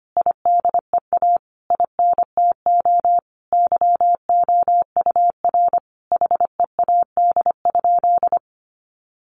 Morse code, heard as a keyed tone, about 25 wpm, 700 Hz. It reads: IDEA INTO YOUR HEAD?
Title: Sentences from Top 600 Words 25wpm